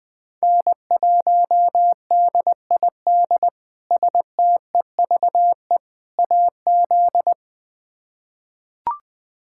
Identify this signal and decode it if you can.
Morse code, heard as a keyed tone, about 21 wpm, 700 Hz. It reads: N1DID STEVE AZ
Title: K1USN SST QSO Practice - 20wpm